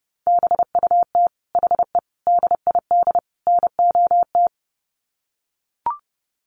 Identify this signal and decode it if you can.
Morse code, heard as a keyed tone, about 30 wpm, 700 Hz. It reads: BUT HE DID NOT
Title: Sets of 4 Words 30wpm